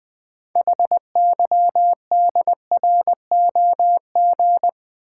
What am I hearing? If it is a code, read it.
Morse code, heard as a keyed tone, about 20 wpm, 700 Hz. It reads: HYDROG
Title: Build A Word - 2 Letters At A Time - 8-Letter Words - 20wpm